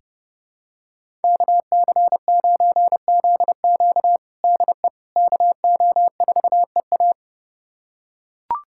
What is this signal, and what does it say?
Morse code, heard as a keyed tone, about 32 wpm, 700 Hz. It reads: KC9ZQ DE KO4EA
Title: USA Callsign DE Callsign - 5 Characters - 30wpm